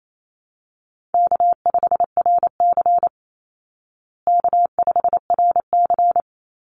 Morse code, heard as a keyed tone, about 28 wpm, 700 Hz. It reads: K5RC K5RC
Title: Call Signs Repeated 2x - 28wpm